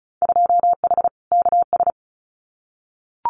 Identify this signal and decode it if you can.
Morse code, heard as a keyed tone, about 34 wpm, 700 Hz. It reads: 2H KS
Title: Winter Field Day Exchanges Encore 35wpm